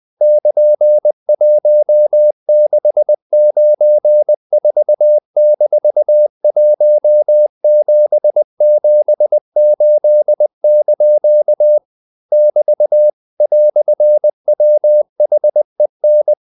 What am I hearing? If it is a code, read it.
Morse code, heard as a keyed tone, about 20 wpm, 600 Hz. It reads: <KN>1694-1778) <BT> "WHEN
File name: CW / Practice / EnglishQuotations0320